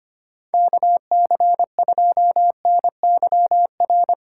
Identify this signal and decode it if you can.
Morse code, heard as a keyed tone, about 25 wpm, 700 Hz. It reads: KC2NYR
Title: USA Call Signs - 6 Characters - 6x Character Spacing - 25wpm